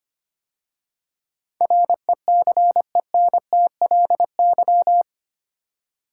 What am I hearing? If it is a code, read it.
Morse code, heard as a keyed tone, about 25 wpm, 700 Hz. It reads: RECENTLY
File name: 8 Letter Words - First 3 Letters Given - 25wpm